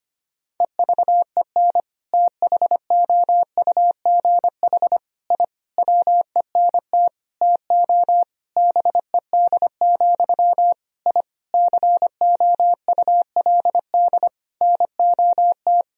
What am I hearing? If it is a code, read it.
Morse code, heard as a keyed tone, about 25 wpm, 700 Hz. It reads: EVEN THOUGH I WENT TO BED, I COULD NOT
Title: Sentences from Top 1000 Words - Review Part 2 - 25wpm